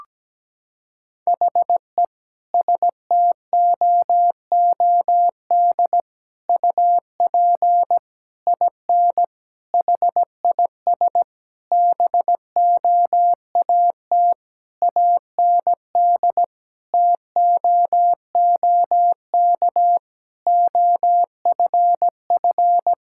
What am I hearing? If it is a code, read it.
Morse code, heard as a keyed tone, about 17 wpm, 700 Hz. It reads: HE STOOD UP IN HIS BOAT AND TOOK OFF
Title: Sentences from Top 1600 Words 17wpm